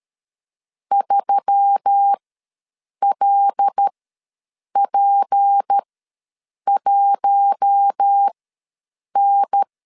Morse code, heard as a keyed tone, about 13 wpm, spaced as for 7 wpm, 800 Hz. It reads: 3LP1N
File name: LEZIONI / 6 PKX0 / 6 P ws 12wpm 3117